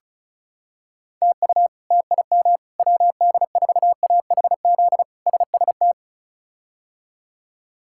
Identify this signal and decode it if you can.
Morse code, heard as a keyed tone, about 34 wpm, 700 Hz. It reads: TU TIM WD4AHZ SST
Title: K1USN SST QSO Practice - 35wpm